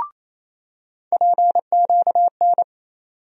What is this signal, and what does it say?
Morse code, heard as a keyed tone, about 28 wpm, 700 Hz. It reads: PQN
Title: USA Call Sign Postfixes - 3 Letters 28wpm